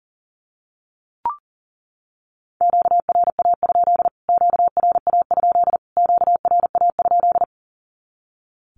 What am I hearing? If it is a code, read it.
Morse code, heard as a keyed tone, about 40 wpm, 700 Hz. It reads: QRA? QRA? QRA?
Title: Q-Codes Repeated 3x 40wpm